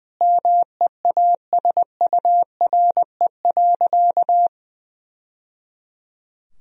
Morse code, heard as a keyed tone, about 21 wpm, 700 Hz. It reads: MEASURE.
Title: Sentences from Top 1000 Words - Review Part 3 - 20wpm